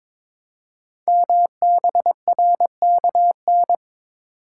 Morse code, heard as a keyed tone, about 22 wpm, 700 Hz. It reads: MBRKN
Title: Code Groups - 5 Letters - 22wpm